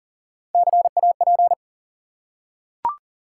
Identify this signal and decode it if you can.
Morse code, heard as a keyed tone, about 40 wpm, 700 Hz. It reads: CAP
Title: Three Letter Words - 40wpm